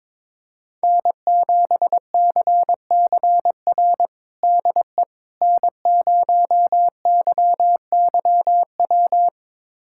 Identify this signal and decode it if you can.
Morse code, heard as a keyed tone, about 22 wpm, 700 Hz. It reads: N7CCR DE N0YYW
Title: USA Callsign DE Callsign - 5 Characters - 22wpm